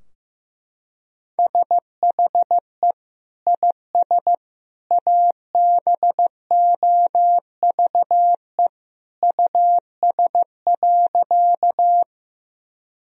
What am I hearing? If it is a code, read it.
Morse code, heard as a keyed tone, about 16 wpm, 700 Hz. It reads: SHE IS ABOVE US.
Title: Sentences from Top 1000 Words - Review Part 2 - 15wpm